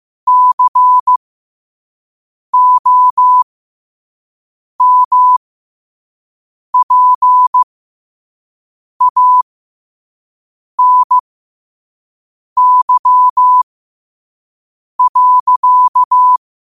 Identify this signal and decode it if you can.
Morse code, heard as a keyed tone, about 15 wpm, spaced as for 5 wpm, 1000 Hz. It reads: COMPANY.
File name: qotdfiles / 5wpm